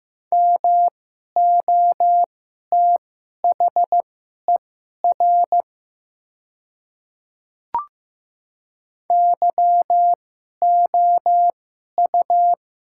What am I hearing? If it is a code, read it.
Morse code, heard as a keyed tone, about 15 wpm, spaced as for 11 wpm, 700 Hz. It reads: MOTHER YOU
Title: 2x Character Spacing - Top 200 Words - 15wpm